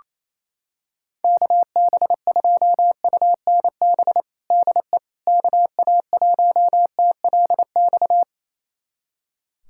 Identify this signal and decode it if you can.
Morse code, heard as a keyed tone, about 28 wpm, 700 Hz. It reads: KB2UNB DE KA1TLX
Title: USA Callsign DE Callsign - 6 Characters - 28wpm